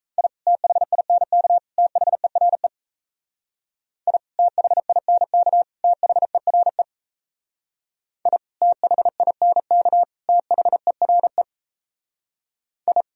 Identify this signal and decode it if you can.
Morse code, heard as a keyed tone, about 35 wpm, 700 Hz. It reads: I THINK THERE I THINK THERE I THINK THERE I
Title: Speed-Racing - Sets of 3 Words 28wpm